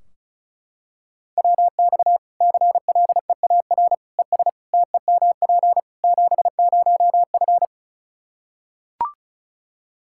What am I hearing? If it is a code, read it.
Morse code, heard as a keyed tone, about 36 wpm, 700 Hz. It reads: WX CLEAR ES TEMP 70F
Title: Practice QSO - Exchange 2 - Info - 35wpm